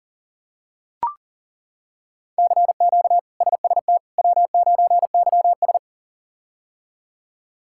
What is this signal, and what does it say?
Morse code, heard as a keyed tone, about 40 wpm, 700 Hz. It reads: CQ SST W9YS
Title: K1USN SST QSO Practice - 40wpm